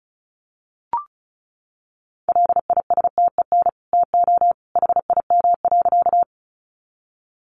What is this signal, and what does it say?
Morse code, heard as a keyed tone, about 34 wpm, 700 Hz. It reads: LISTEN TO HIM.
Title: Sentences from Top 400 Words 35wpm